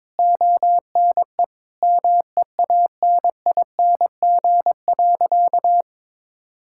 Morse code, heard as a keyed tone, about 22 wpm, 700 Hz. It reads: ONE MEANING.
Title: Sentences from Top 1000 Words - Review Part 6 - 22wpm